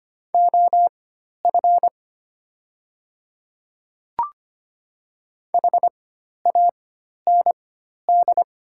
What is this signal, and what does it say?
Morse code, heard as a keyed tone, about 25 wpm, spaced as for 12 wpm, 700 Hz. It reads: OF HAND
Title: 4x Character Spacing - Top 500 Words - 25wpm